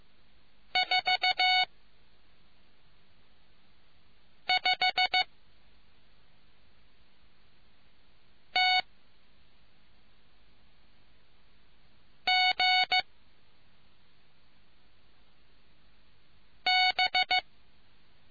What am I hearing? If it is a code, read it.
Morse code, heard as a keyed tone, about 15 wpm, spaced as for 3 wpm, 2200 Hz. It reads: 45TGB